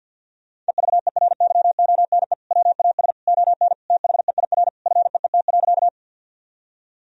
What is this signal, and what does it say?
Morse code, heard as a keyed tone, about 50 wpm, 700 Hz. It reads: EVERYONE WAS ON THEIR FEET.
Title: Sentences from Top 1000 Words - Review Part 1 - 50wpm